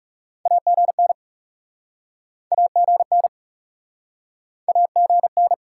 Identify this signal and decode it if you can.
Morse code, heard as a keyed tone, about 45 wpm, 700 Hz. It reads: AGN AGN AGN